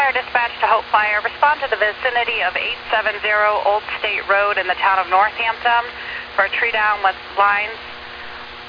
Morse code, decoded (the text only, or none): none